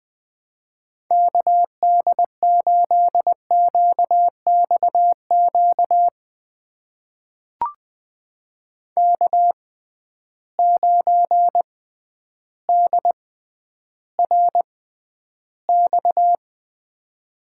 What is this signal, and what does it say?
Morse code, heard as a keyed tone, about 20 wpm, spaced as for 11 wpm, 700 Hz. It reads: KD8QXQ K9DRX
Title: USA Call Signs - 6x Character Spacing - 20wpm